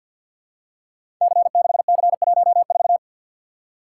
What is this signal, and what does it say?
Morse code, heard as a keyed tone, about 53 wpm, 700 Hz. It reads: C6C14
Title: Code Groups - 5 Letter:Numbers - 50wpm